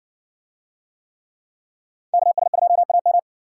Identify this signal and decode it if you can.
Morse code, heard as a keyed tone, about 60 wpm, 700 Hz. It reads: KS2AM